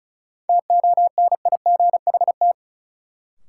Morse code, heard as a keyed tone, about 34 wpm, 700 Hz. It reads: TONIGHT
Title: Sentences from Top 600 Words - New Word Drill - 35wpm